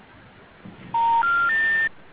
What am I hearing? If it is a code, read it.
Special information tone.